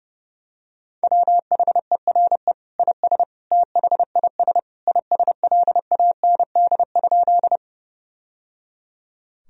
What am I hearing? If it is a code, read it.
Morse code, heard as a keyed tone, about 30 wpm, 700 Hz. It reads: WHERE IS THIS ISLAND?
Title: Sentences from Top 1000 Words - Review Part 3 - 30wpm